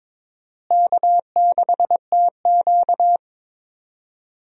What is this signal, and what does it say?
Morse code, heard as a keyed tone, about 22 wpm, 700 Hz. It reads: K6TQ
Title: USA Call Signs - 4 Characters - Repeated 3x - 22wpm